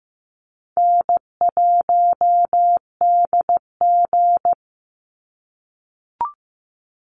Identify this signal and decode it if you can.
Morse code, heard as a keyed tone, about 15 wpm, 700 Hz. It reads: N1DG
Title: Easy Call Signs - 15wpm